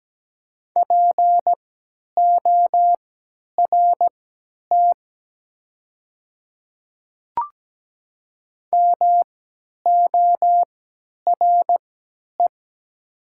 Morse code, heard as a keyed tone, about 17 wpm, spaced as for 10 wpm, 700 Hz. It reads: PORT MORE
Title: 3x Character Spacing - Top 200 Words - 17wpm